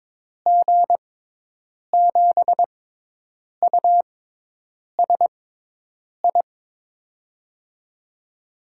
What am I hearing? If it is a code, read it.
Morse code, heard as a keyed tone, about 22 wpm, spaced as for 8 wpm, 700 Hz. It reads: G7USI
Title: USA Call Signs - 6 Characters - 6x Character Spacing - 22wpm